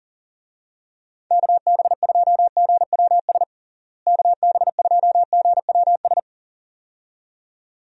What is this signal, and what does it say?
Morse code, heard as a keyed tone, about 40 wpm, 700 Hz. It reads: KB2GWS KB2GWS
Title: USA Call Signs - 6 Characters - Repeated 2x - 40wpm